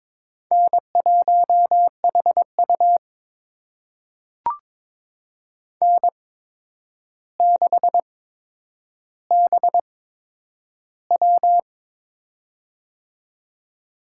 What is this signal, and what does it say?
Morse code, heard as a keyed tone, about 22 wpm, spaced as for 9 wpm, 700 Hz. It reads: N1HU N6BW
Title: USA Call Signs - 4 Characters - 8x Character Spacing - 22wpm